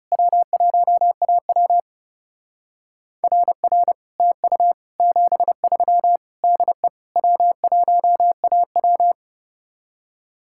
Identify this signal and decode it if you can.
Morse code, heard as a keyed tone, about 36 wpm, 700 Hz. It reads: W1AW RR TU 73 DE W1AW
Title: Speed-Racing Common QSO Phrases - 30wpm